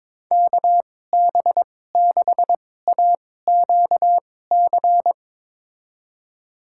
Morse code, heard as a keyed tone, about 22 wpm, spaced as for 16 wpm, 700 Hz. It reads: KB6AQC
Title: USA Call Signs - 6 Characters - 2x Character Spacing - 22wpm